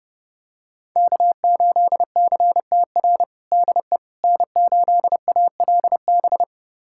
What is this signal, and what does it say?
Morse code, heard as a keyed tone, about 30 wpm, 700 Hz. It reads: K8CTR DE N8ALB
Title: USA Callsign DE Callsign - 5 Characters - 30wpm